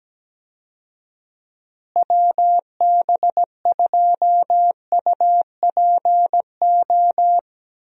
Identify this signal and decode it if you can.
Morse code, heard as a keyed tone, about 17 wpm, 700 Hz. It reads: WB2UPO
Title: USA Call Signs - 6 Characters - Repeated 3x - 17wpm